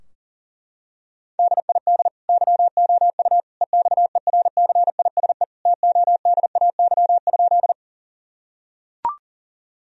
Morse code, heard as a keyed tone, about 40 wpm, 700 Hz. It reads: DID YOU EXERCISE TODAY?
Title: Sentences from Top 1000 Words - Review Part 4 - 40wpm